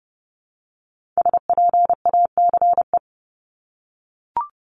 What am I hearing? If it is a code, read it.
Morse code, heard as a keyed tone, about 30 wpm, 700 Hz. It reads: SPACE
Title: Multi-letter Phonemes Repeated 3x 30wpm